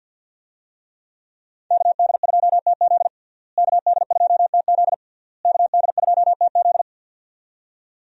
Morse code, heard as a keyed tone, about 53 wpm, 700 Hz. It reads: KD2TZ KD2TZ KD2TZ
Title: USA Call Signs - 5 Characters - Repeated 3x - 50wpm